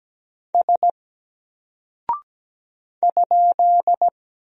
Morse code, heard as a keyed tone, about 17 wpm, 700 Hz. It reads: S ?
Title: Single Character (Letter:Number:Punctuation:Prosign) Alt 17wpm